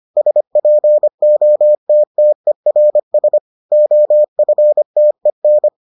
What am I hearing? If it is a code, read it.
Morse code, heard as a keyed tone, about 25 wpm, 600 Hz. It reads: SPOTTERS OFTEN